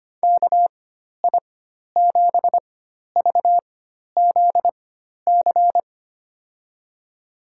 Morse code, heard as a keyed tone, about 25 wpm, spaced as for 12 wpm, 700 Hz. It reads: KI7VZC
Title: USA Call Signs - 6 Characters - 4x Character Spacing - 25wpm